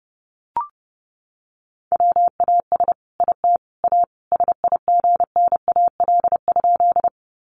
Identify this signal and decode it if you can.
Morse code, heard as a keyed tone, about 30 wpm, 700 Hz. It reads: WAS IT A SIGNAL?
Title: Sentences from Top 1400 Words 30wpm